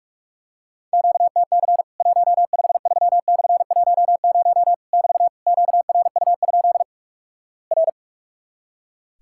Morse code, heard as a keyed tone, about 46 wpm, 700 Hz. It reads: QTC 153/10 <BT> QRU? R
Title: Worked All Europe DX Contest - QTC Exchange - 45wpm